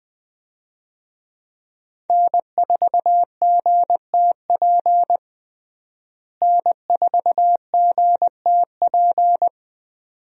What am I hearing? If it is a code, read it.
Morse code, heard as a keyed tone, about 20 wpm, 700 Hz. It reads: N4GTP N4GTP
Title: USA Call Signs - 5 Characters - Repeated 2x - 20wpm